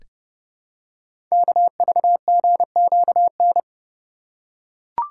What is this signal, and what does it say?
Morse code, heard as a keyed tone, about 31 wpm, 690 Hz. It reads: KVGQN
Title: Code Groups - 5 Letters - 30wpm